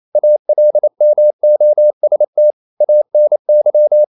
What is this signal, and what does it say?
Morse code, heard as a keyed tone, about 28 wpm, 600 Hz. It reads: ALMOST ANY
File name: CW / Practice / EnglishQuotations0228